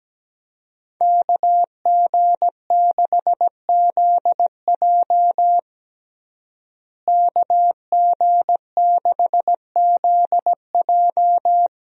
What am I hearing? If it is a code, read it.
Morse code, heard as a keyed tone, about 17 wpm, 700 Hz. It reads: KG6ZJ KG6ZJ
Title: USA Call Signs - 5 Characters - Repeated 3x - 17wpm